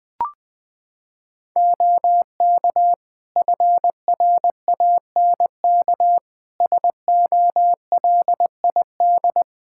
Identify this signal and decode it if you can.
Morse code, heard as a keyed tone, about 20 wpm, 700 Hz. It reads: OK FRANK SOLID
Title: Practice QSO - Exchange 3 - More Info - 20wpm